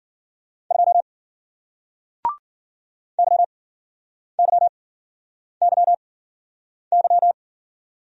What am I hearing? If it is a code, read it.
Morse code, heard as a keyed tone, about 53 wpm, spaced as for 9 wpm, 700 Hz. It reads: 2 YYYY